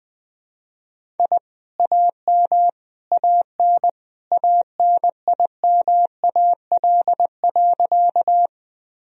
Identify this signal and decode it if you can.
Morse code, heard as a keyed tone, about 20 wpm, 700 Hz. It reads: I AM AN ANIMAL.
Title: Sentences from Top 1000 Words - Review Part 1 - 20wpm